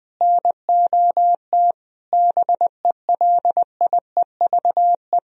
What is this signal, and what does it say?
Morse code, heard as a keyed tone, about 20 wpm, 700 Hz. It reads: NOT BELIEVE
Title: Sentences from Top 1000 Words - Review Part 3 - 20wpm